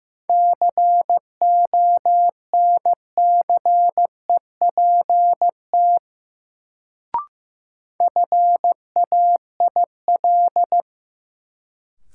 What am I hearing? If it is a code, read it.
Morse code, heard as a keyed tone, about 15 wpm, 700 Hz. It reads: CONCEPT FAIL
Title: Sentences from Top 1100 Words - New Word Drill - 15wpm